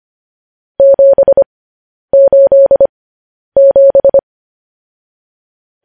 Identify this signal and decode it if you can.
Morse code, heard as a keyed tone, about 25 wpm, spaced as for 10 wpm, 550 Hz. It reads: 787
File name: podcast / News headlines-1x-25x10